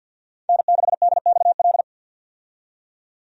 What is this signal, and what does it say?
Morse code, heard as a keyed tone, about 52 wpm, 700 Hz. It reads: N6DXL
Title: USA Call Signs - 5 Characters - 50wpm